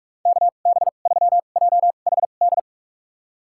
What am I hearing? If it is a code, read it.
Morse code, heard as a keyed tone, about 46 wpm, spaced as for 32 wpm, 700 Hz. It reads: KB3JHD
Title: USA Call Signs - 6 Characters - 2x Character Spacing - 45wpm